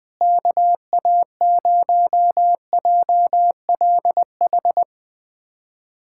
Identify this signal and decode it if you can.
Morse code, heard as a keyed tone, about 20 wpm, 700 Hz. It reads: KA0JLH
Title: USA Callsign DE Callsign - 6 Characters - 20wpm